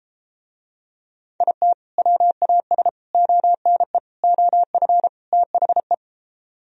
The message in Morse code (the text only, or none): IT WAS ONE OF THE